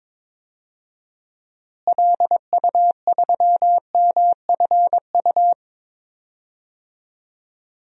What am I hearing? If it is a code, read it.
Morse code, heard as a keyed tone, about 22 wpm, 700 Hz. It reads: LU3MFU
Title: DXCC-100 Call Signs - 6 Characters - Repeated 2x - 22wpm